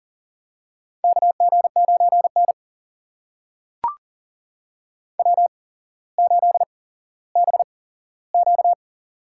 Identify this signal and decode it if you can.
Morse code, heard as a keyed tone, about 42 wpm, spaced as for 17 wpm, 700 Hz. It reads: KG9N W8BQ
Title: USA Call Signs - 4 Characters - 8x Character Spacing - 40wpm